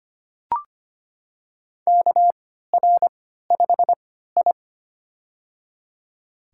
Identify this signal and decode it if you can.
Morse code, heard as a keyed tone, about 25 wpm, spaced as for 14 wpm, 700 Hz. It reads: KR5I
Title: USA Call Signs - 4 Characters - 3x Character Spacing - 25wpm